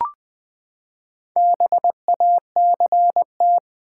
Morse code, heard as a keyed tone, about 20 wpm, 700 Hz. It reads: BACT
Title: Build A Word - 4 Letters At A Time - 8-Letter Words - 20wpm